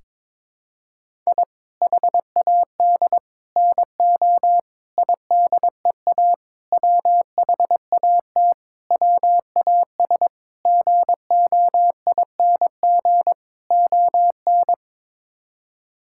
Morse code, heard as a keyed tone, about 21 wpm, 700 Hz. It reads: I HAD NO IDEA WHAT WAS GOING ON
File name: practice / 22wpm / Sets of 8 Words - 22wpm